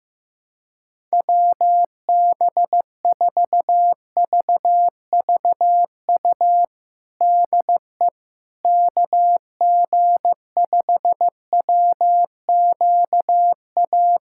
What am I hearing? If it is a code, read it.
Morse code, heard as a keyed tone, about 16 wpm, 700 Hz. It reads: WB4VVU DE KG5WQA
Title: USA Callsign DE Callsign - 6 Characters - 15wpm